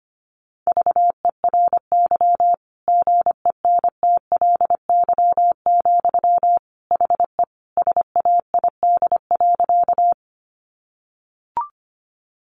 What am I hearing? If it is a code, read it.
Morse code, heard as a keyed tone, about 25 wpm, 700 Hz. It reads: VERY GENTLY, HE SAID.
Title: Sentences from Top 1500 Words 25wpm